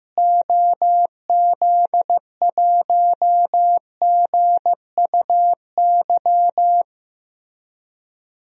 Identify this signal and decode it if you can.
Morse code, heard as a keyed tone, about 15 wpm, 700 Hz. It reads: OZ1GUY
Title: DXCC-100 Call Signs - 6 Characters - Repeated 3x - 15wpm